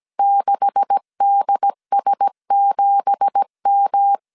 Morse code, heard as a keyed tone, about 17 wpm, 800 Hz. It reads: 6DS7M
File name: CW / DET / 1 9 1500C S A 16wpm 2828